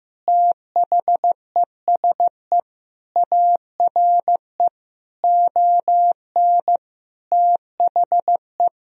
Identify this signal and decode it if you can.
Morse code, heard as a keyed tone, about 16 wpm, 700 Hz. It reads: THESE ARE ON THE
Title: Sentences from Top 1000 Words - Review Part 3 - 15wpm